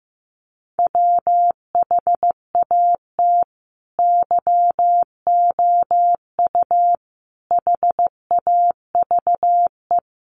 Morse code, heard as a keyed tone, about 15 wpm, 700 Hz. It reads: WHAT YOU HAVE
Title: Sets of 5 Words Encore 15wpm